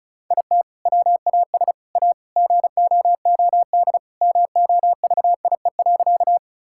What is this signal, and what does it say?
Morse code, heard as a keyed tone, about 34 wpm, 700 Hz. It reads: IT WAS A GOOD MOVIE.
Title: Sentences from Top 1000 Words - Review Part 5 - 35wpm